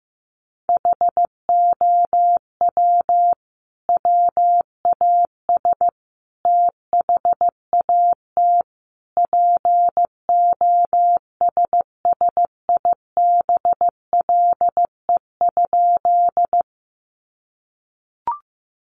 Morse code, heard as a keyed tone, about 15 wpm, 700 Hz. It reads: HOW WAS THAT POSSIBLE?
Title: Sentences from Top 500 Words 15wpm